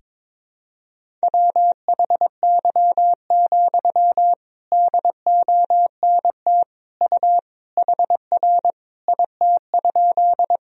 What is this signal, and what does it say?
Morse code, heard as a keyed tone, about 21 wpm, 700 Hz. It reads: WHY, DONT U HR IT?
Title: Sentences from Top 300 Words and Abbreviations 22wpm